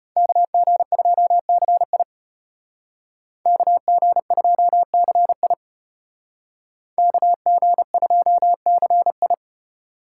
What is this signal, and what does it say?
Morse code, heard as a keyed tone, about 32 wpm, 700 Hz. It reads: KG2CI KG2CI KG2CI